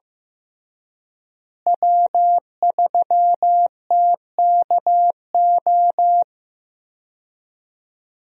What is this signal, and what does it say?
Morse code, heard as a keyed tone, about 15 wpm, 700 Hz. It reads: W3TKO
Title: USA Call Signs - 5 Characters - Repeated 3x - 15wpm